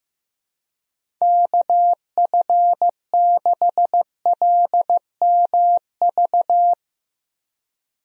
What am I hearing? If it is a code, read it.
Morse code, heard as a keyed tone, about 16 wpm, 700 Hz. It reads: KF6LMV
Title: USA Call Signs - 6 Characters - Repeated 3x - 15wpm